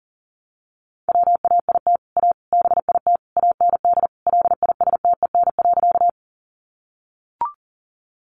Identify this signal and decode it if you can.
Morse code, heard as a keyed tone, about 40 wpm, 700 Hz. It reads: WAIT A BIT AND LISTEN.
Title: Sentences from Top 900 Words 40wpm